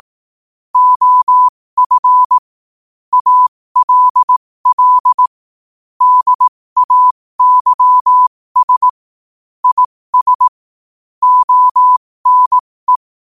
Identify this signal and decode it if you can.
Morse code, heard as a keyed tone, about 18 wpm, spaced as for 15 wpm, 1000 Hz. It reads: OF ALL DAYS IS ONE
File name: 15wpm